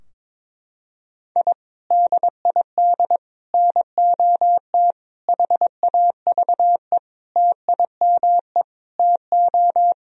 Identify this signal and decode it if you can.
Morse code, heard as a keyed tone, about 22 wpm, 700 Hz. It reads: I DID NOT HAVE TIME TO
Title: Sentences from Top 1000 Words - Review Part 3 - 22wpm